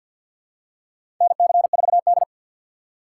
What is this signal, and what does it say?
Morse code, heard as a keyed tone, about 53 wpm, 690 Hz. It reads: NC4D
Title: USA Call Signs - 4 Characters - 6x Character Spacing - 50wpm